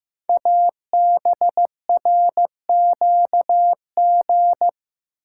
Morse code, heard as a keyed tone, about 15 wpm, 700 Hz. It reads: ABRQG
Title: Code Groups - 5 Letters - 15wpm